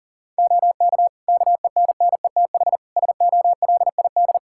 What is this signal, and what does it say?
Morse code, heard as a keyed tone, about 40 wpm, 700 Hz. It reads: OK KENNETH SOLID
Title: Practice QSO - Exchange 3 - More Info - 40wpm